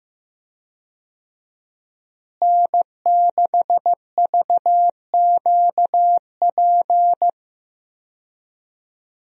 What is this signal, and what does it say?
Morse code, heard as a keyed tone, about 15 wpm, 700 Hz. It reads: N6VQP
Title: USA Call Signs - 5 Characters - Repeated 3x - 15wpm